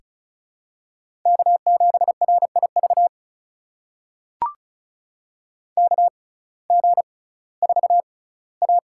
Morse code, heard as a keyed tone, about 34 wpm, spaced as for 19 wpm, 700 Hz. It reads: K7RIV KG4A
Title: USA Call Signs - 6x Character Spacing - 35wpm